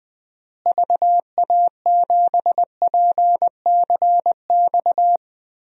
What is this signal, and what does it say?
Morse code, heard as a keyed tone, about 21 wpm, 700 Hz. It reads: VA7PCX
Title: DXCC-100 Call Signs - 6 Characters - Repeated 3x - 20wpm